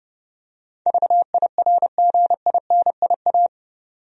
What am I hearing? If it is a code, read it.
Morse code, heard as a keyed tone, about 30 wpm, 700 Hz. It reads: VIRGINIA
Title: USA Common Names - 30wpm